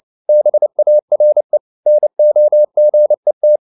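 Morse code, heard as a keyed tone, about 29 wpm, 600 Hz. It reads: BARE NOGET
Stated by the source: Fast (QRQ) Morse at 29 wpm, from CW med Gnister.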